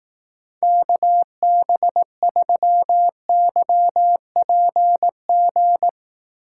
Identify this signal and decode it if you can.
Morse code, heard as a keyed tone, about 18 wpm, 700 Hz. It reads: KB3YPG
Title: Speed Racing - USA Call Signs 15wpm